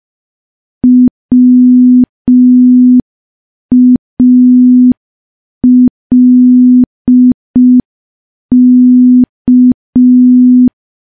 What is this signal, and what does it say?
Morse code, heard as a keyed tone, about 5 wpm, 250 Hz. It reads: WALK
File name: morse / morse-training-05WPM